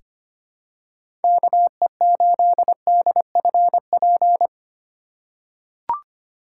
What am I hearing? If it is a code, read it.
Morse code, heard as a keyed tone, about 25 wpm, 700 Hz. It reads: KE8DFP
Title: USA Call Signs - 4x Character Spacing - 25wpm